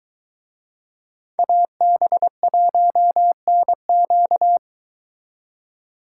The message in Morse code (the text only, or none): AB1NQ